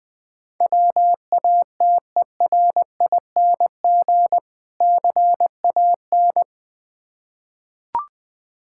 Morse code, heard as a keyed tone, about 21 wpm, 700 Hz. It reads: WATERING CAN
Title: Difficult Words with Hints - 20wpm